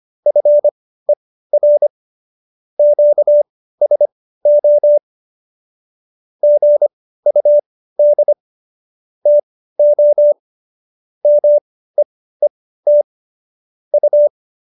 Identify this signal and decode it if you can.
Morse code, heard as a keyed tone, about 25 wpm, spaced as for 15 wpm, 590 Hz. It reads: FER QSO GUD TO MEET U